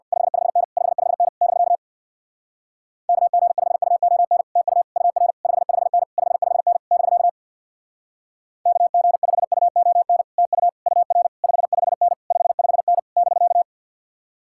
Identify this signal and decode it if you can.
Morse code, heard as a keyed tone, about 58 wpm, 700 Hz. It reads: 55N 55N <BK> KG5UON TU UR 55N 55N <BK> KG5UON TU UR 55N 55N <BK>
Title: Speed-Racing Common QSO Phrases - 50wpm